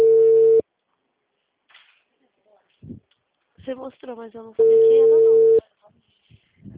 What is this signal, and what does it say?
Ringback tone (China, 450 Hz), with speech.
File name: out- -2050- - - .62934